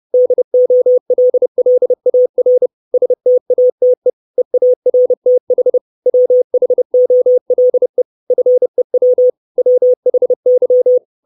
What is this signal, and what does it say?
Morse code, heard as a keyed tone, about 30 wpm, 500 Hz. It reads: DOLLAR STATE EARTH WHOLE FEW WHY
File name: 632 Words 30WPM